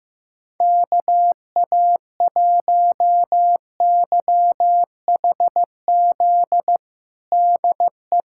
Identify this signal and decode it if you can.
Morse code, heard as a keyed tone, about 16 wpm, 700 Hz. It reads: KA1YHZ DE
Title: USA Callsign DE Callsign - 6 Characters - 15wpm